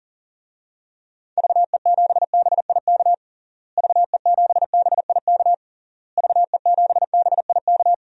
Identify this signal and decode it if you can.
Morse code, heard as a keyed tone, about 40 wpm, 700 Hz. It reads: VE7BIK VE7BIK VE7BIK
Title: DXCC-100 Call Signs - 6 Characters - Repeated 3x - 40wpm